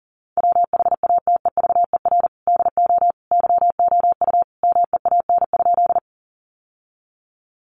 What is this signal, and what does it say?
Morse code, heard as a keyed tone, about 40 wpm, 700 Hz. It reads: WHATEVER DO YOU MEAN?
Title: Sentences from Top 1100 Words 40wpm